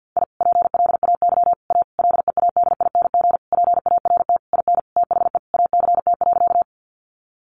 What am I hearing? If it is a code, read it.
Morse code, heard as a keyed tone, about 49 wpm, 700 Hz. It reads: I PLAY A LEADING PART IN THE ACT.
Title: Sentences from Top 1100 Words 50wpm